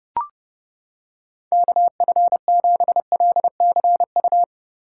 Morse code, heard as a keyed tone, about 28 wpm, 700 Hz. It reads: KF7LCU